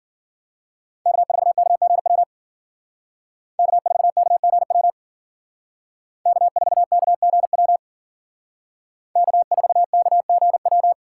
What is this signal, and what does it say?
Morse code, heard as a keyed tone, about 52 wpm, 700 Hz. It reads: K4KGW K4KGW K4KGW K4KGW